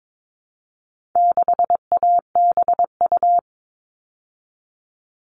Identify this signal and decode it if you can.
Morse code, heard as a keyed tone, about 22 wpm, 700 Hz. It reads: 6ABU
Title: All Cut Numbers - 4-Digit 22wpm